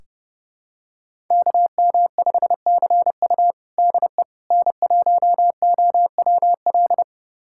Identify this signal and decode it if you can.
Morse code, heard as a keyed tone, about 30 wpm, 700 Hz. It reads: KM5CU DE N1OWL
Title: USA Callsign DE Callsign - 5 Characters - 30wpm